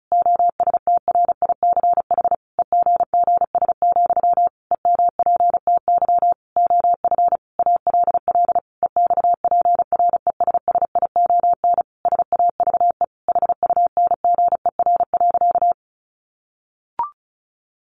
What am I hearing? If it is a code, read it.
Morse code, heard as a keyed tone, about 34 wpm, 700 Hz. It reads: OSTRICH EGGS, EMPTY OF ALL EXPRESSION SAVE HUNGER.
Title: A Sound Of Thunder 35wpm